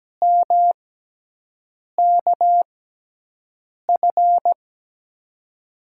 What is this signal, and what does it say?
Morse code, heard as a keyed tone, about 17 wpm, spaced as for 6 wpm, 700 Hz. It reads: MKF
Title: USA Call Signs - 5 Characters - 6x Character Spacing - 17wpm